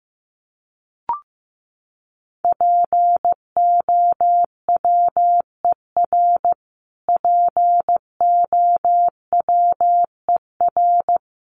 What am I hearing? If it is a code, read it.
Morse code, heard as a keyed tone, about 15 wpm, 700 Hz. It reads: POWER POWER
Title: Top 500 Words Repeated 15wpm